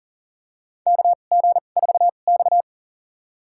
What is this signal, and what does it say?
Morse code, heard as a keyed tone, about 40 wpm, spaced as for 29 wpm, 700 Hz. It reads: KG4X
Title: USA Call Signs - 4 Characters - 2x Character Spacing - 40wpm